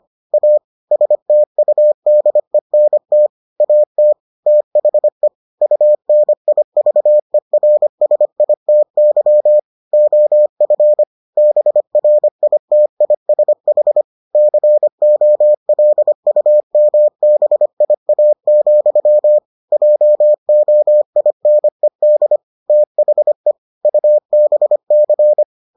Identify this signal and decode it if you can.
Morse code, heard as a keyed tone, about 25 wpm, 600 Hz. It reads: A STUDENT AT THE UNIVERSITY OF BRITISH COLUMBIA, JOINED THE UBC